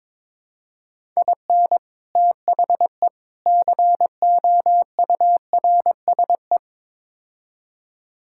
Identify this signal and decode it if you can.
Morse code, heard as a keyed tone, about 22 wpm, 700 Hz. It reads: IN THE COURSE
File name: practice / 22wpm / Build A Phrase - 5-Word Phrases - 22wpm